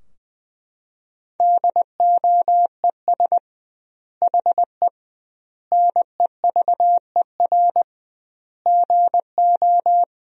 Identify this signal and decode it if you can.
Morse code, heard as a keyed tone, about 20 wpm, 700 Hz. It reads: DOES HE NEVER GO
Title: 2x Word-Spacing - Sentences from Top 300 Words - 20wpm